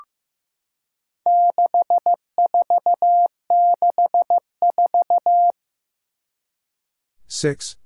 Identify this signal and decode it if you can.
Morse code, heard as a keyed tone, about 15 wpm, 700 Hz. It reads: 6464
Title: Four Challenging Letter-Number Combinations 15wpm